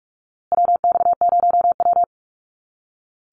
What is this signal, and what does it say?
Morse code, heard as a keyed tone, about 44 wpm, 700 Hz. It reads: WX0W
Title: USA Call Signs Repeated 2x 45wpm